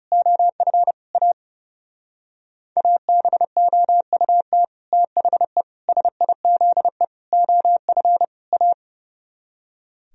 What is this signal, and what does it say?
Morse code, heard as a keyed tone, about 35 wpm, 700 Hz. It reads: OF A ABOUT THE SIZE OF A
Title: Speed-Racing - Sets of 5 Words 30wpm